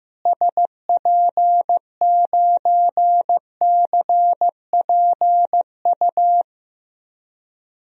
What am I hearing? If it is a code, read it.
Morse code, heard as a keyed tone, about 15 wpm, 700 Hz. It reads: SP9CPU
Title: DXCC-100 Call Signs - 6 Characters - Repeated 3x - 15wpm